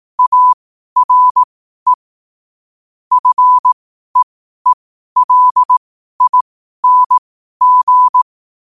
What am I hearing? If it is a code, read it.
Morse code, heard as a keyed tone, about 18 wpm, spaced as for 12 wpm, 1000 Hz. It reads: ARE FEELING
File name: qotdfiles / 12wpm